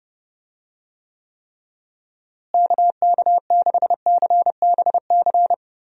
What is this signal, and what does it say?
Morse code, heard as a keyed tone, about 32 wpm, 700 Hz. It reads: KK6CBC
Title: USA Call Signs - 6 Characters - Repeated 3x - 30wpm